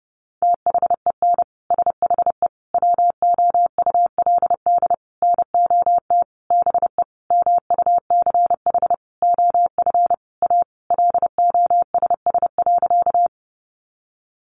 Morse code, heard as a keyed tone, about 30 wpm, 700 Hz. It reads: THEN SHE WOULD NOT BE MUCH OF A LOSS.
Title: Sentences from Top 1400 Words 30wpm